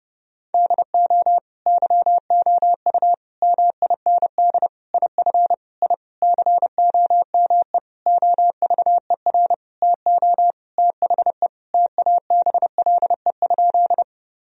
Morse code, heard as a keyed tone, about 30 wpm, 700 Hz. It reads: DO YOU MIND IF I COME OVER TO THE TABLE?
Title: Sentences from Top 1000 Words - Review Part 2 - 30wpm